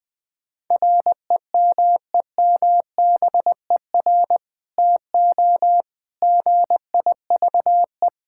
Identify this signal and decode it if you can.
Morse code, heard as a keyed tone, about 20 wpm, 700 Hz. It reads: REMEMBER TO GIVE
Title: Sentences with 2-Word Phrases - 20wpm